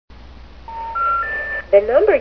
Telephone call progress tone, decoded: special information tone